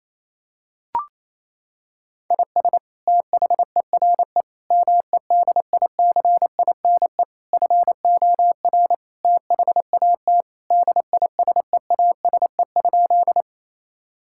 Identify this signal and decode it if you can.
Morse code, heard as a keyed tone, about 27 wpm, 700 Hz. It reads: IS THERE MEDICINE FOR THAT DISEASE?
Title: Sentences from Top 1000 Words - Review Part 6 - 28wpm